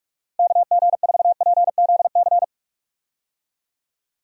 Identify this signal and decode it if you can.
Morse code, heard as a keyed tone, about 46 wpm, 700 Hz. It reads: KG4PZC
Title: USA Call Signs - 6 Characters - Repeated 3x - 45wpm